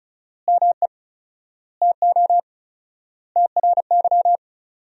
Morse code, heard as a keyed tone, about 36 wpm, 700 Hz. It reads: ME TO TRY
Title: 4x Word-Spacing - Sentences from Top 400 Words - 35wpm